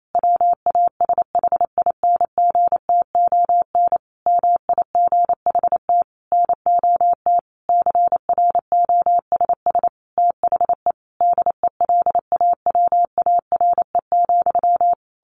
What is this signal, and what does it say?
Morse code, heard as a keyed tone, about 28 wpm, 700 Hz. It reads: WASHINGTON MIGHT NOT CROSS THE DELAWARE,
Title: A Sound Of Thunder 28wpm